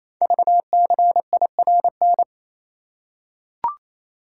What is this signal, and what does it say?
Morse code, heard as a keyed tone, about 27 wpm, 690 Hz. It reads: VCIRN
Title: Code Groups - 5 Letters - 28wpm